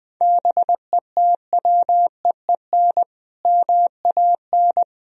Morse code, heard as a keyed tone, about 20 wpm, 700 Hz. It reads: BETWEEN MAN
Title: Sentences from Top 1000 Words - Review Part 1 - 20wpm